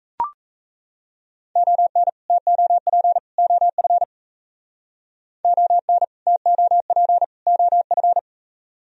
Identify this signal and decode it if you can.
Morse code, heard as a keyed tone, about 34 wpm, 700 Hz. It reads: ON TOP OF ON TOP OF
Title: Speed-Racing - Sets of 3 Words 28wpm